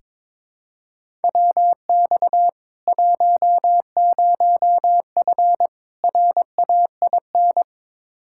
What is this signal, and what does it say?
Morse code, heard as a keyed tone, about 22 wpm, 700 Hz. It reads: WX 10F RAIN
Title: K1USN SST QSO Practice 22wpm